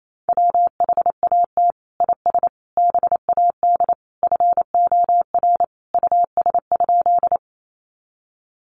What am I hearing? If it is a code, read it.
Morse code, heard as a keyed tone, about 28 wpm, 700 Hz. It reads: WHAT IS BAD FOR US?
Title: Sentences from Top 700 Words 28wpm